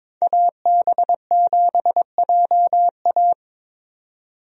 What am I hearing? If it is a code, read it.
Morse code, heard as a keyed tone, about 22 wpm, 700 Hz. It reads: AB7JA
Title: USA Call Signs - 5 Characters - 22wpm